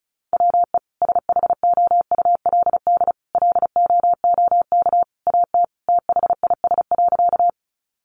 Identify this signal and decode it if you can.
Morse code, heard as a keyed tone, about 34 wpm, 700 Hz. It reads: WE SHOULD LOOK AT THIS.
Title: Sentences from Top 300 Words 35wpm